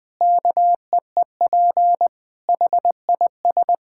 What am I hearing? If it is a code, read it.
Morse code, heard as a keyed tone, about 20 wpm, 700 Hz. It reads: KEEP HIS
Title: Sentences from Top 1000 Words - Review Part 6 - 20wpm